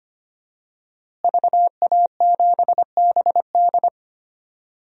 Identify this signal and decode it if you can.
Morse code, heard as a keyed tone, about 25 wpm, 700 Hz. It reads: VA7BD
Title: DXCC-100 Call Signs - 5 Characters - Repeated 2x - 25wpm